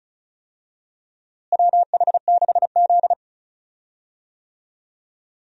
Morse code, heard as a keyed tone, about 36 wpm, 700 Hz. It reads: WH6Z
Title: USA Call Signs - 4 Characters - Repeated 2x - 35wpm